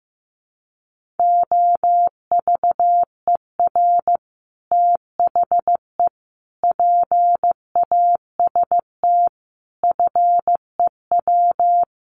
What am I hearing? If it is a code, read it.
Morse code, heard as a keyed tone, about 15 wpm, 700 Hz. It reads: OVER THE PAST FEW
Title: Sets of 5 Words Encore 15wpm